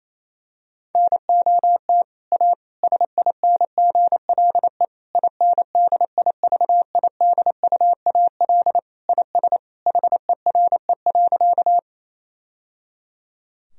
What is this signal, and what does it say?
Morse code, heard as a keyed tone, about 27 wpm, 700 Hz. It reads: NOT A SINGLE INDIVIDUAL IS HERE.
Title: Sentences from Top 1000 Words - Review Part 4 - 28wpm